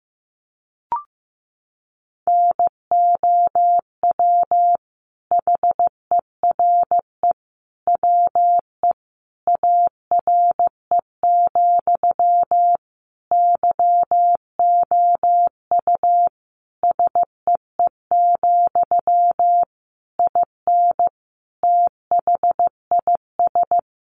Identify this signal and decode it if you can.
Morse code, heard as a keyed tone, about 14 wpm, 700 Hz. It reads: NOW HERE WE ARE, YOU SEE, IN THIS
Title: Sentences from Top 1200 Words 15wpm